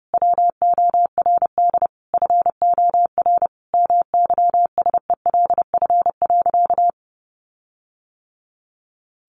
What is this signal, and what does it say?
Morse code, heard as a keyed tone, about 30 wpm, 700 Hz. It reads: WORD FOR MYSELF.
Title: Sentences from Top 1300 Words 30wpm